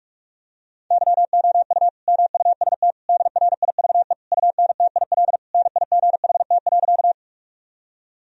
Morse code, heard as a keyed tone, about 42 wpm, 700 Hz. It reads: YOU MUST DRIVE UNTIL NIGHT.
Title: Sentences from Top 1000 Words - Review Part 2 - 45wpm